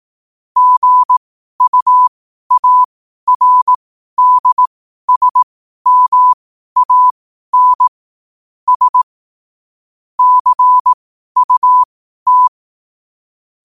Morse code, heard as a keyed tone, about 18 wpm, spaced as for 12 wpm, 1000 Hz. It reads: GUARDSMAN S CUT
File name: qotdfiles / 12wpm